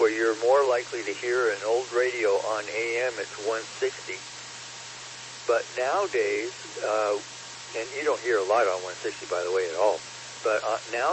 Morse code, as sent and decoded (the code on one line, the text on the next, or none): none